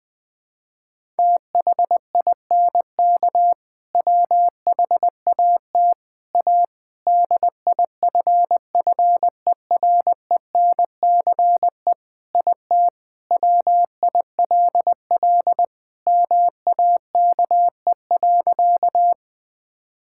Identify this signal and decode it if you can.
Morse code, heard as a keyed tone, about 21 wpm, 700 Hz. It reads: THINK WHAT A DIFFERENCE IT WILL MAKE.
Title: Sentences from Top 1000 Words - Review Part 5 - 20wpm